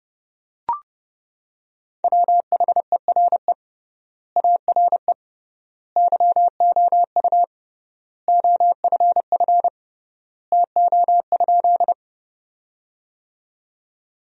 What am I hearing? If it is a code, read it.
Morse code, heard as a keyed tone, about 32 wpm, 700 Hz. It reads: WHERE ARE YOU OFF TO?
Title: 3x Word-Spacing - Sentences from Top 200 Words - 30wpm